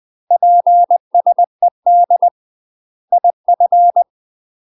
Morse code, heard as a keyed tone, about 20 wpm, 700 Hz. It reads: PSED IF
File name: UK MASTER QSOs TRAINING / 20WPM / UK MASTER QSOs 0012